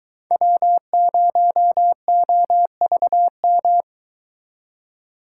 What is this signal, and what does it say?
Morse code, heard as a keyed tone, about 22 wpm, 700 Hz. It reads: W0OVM